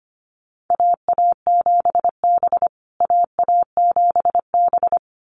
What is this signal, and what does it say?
Morse code, heard as a keyed tone, about 25 wpm, 700 Hz. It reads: AA7B AA7B
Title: USA Call Signs Repeated 2x 25wpm